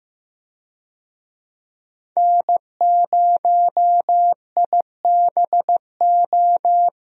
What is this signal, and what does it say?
Morse code, heard as a keyed tone, about 15 wpm, 700 Hz. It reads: N0IBO
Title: USA Call Signs - 5 Characters - Repeated 3x - 15wpm